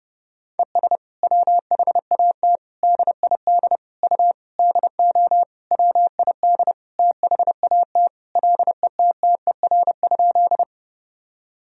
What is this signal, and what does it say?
Morse code, heard as a keyed tone, about 30 wpm, 700 Hz. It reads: ES WHAT DID U DO WID THAT LETTER?
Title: Sentences from Top 300 Words and Abbreviations 30wpm